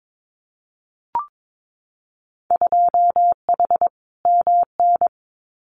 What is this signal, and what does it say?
Morse code, heard as a keyed tone, about 22 wpm, 700 Hz. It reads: 2H MN
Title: Winter Field Day Exchanges 22wpm